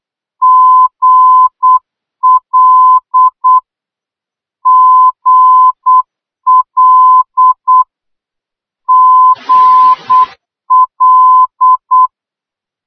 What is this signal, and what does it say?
Morse code, heard as a keyed tone, about 8 wpm, 1020 Hz. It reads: GL GL GL